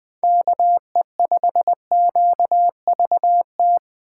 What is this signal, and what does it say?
Morse code, heard as a keyed tone, about 20 wpm, 700 Hz. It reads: KE5QVT
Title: USA Call Signs - 6 Characters - 2x Character Spacing - 20wpm